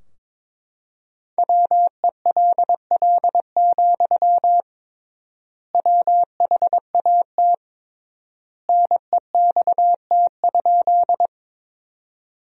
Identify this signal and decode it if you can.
Morse code, heard as a keyed tone, about 22 wpm, 700 Hz. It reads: WELL, WHAT NEXT?
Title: 3x Word-Spacing - Sentences from Top 300 Words - 22wpm